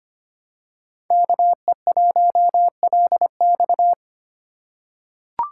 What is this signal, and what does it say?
Morse code, heard as a keyed tone, about 25 wpm, 700 Hz. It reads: KE1LX
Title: USA Call Signs - 5 Characters - 4x Character Spacing - 25wpm